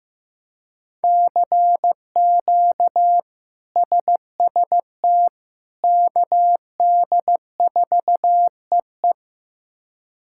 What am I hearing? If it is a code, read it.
Morse code, heard as a keyed tone, about 15 wpm, 700 Hz. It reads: CQ SST KD4EE
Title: K1USN SST QSO Practice - 15wpm